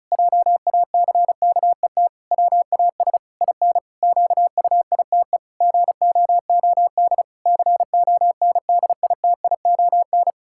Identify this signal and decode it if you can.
Morse code, heard as a keyed tone, about 34 wpm, 700 Hz. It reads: JACKET WAS IN QUITE GOOD CONDITION
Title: Pangram Challenge v2 - 35wpm